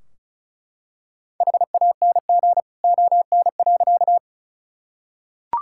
Morse code, heard as a keyed tone, about 36 wpm, 700 Hz. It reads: HANG ON.
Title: Sentences from Top 1600 Words 35wpm